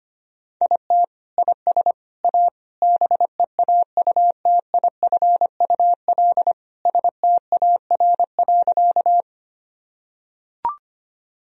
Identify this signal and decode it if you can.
Morse code, heard as a keyed tone, about 25 wpm, 700 Hz. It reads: IT IS A BEAUTIFUL STAR.
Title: Sentences from Top 1000 Words - Review Part 2 - 25wpm